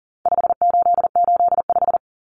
Morse code, heard as a keyed tone, about 40 wpm, 700 Hz. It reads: 5885
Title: Numbers - 4-Digit 40wpm